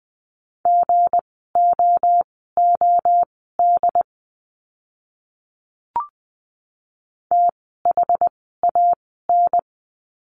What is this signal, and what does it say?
Morse code, heard as a keyed tone, about 20 wpm, spaced as for 15 wpm, 700 Hz. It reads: GOOD THAN
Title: 230 Lesson 37f - Letter x - Words 20wpm